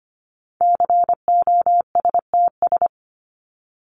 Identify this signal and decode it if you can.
Morse code, heard as a keyed tone, about 25 wpm, 700 Hz. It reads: COSTS
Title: ES:S Suffix - Top 100 Words with Base Words 25wpm